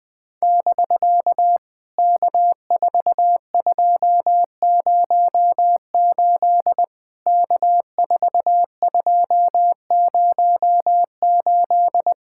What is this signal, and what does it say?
Morse code, heard as a keyed tone, about 20 wpm, 700 Hz. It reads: <BK> K4208 K4208
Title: POTA QSO Practice - 20wpm